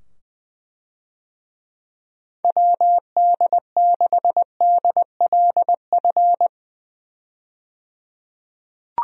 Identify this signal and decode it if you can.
Morse code, heard as a keyed tone, about 20 wpm, 700 Hz. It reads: WD6DLF
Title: USA Call Signs - 6 Characters - 20wpm